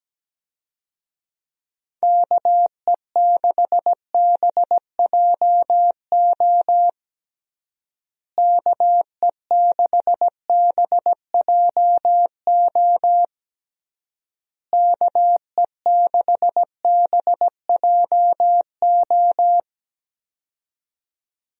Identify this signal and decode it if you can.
Morse code, heard as a keyed tone, about 17 wpm, 700 Hz. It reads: KE6BJO KE6BJO KE6BJO
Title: USA Call Signs - 6 Characters - Repeated 3x - 17wpm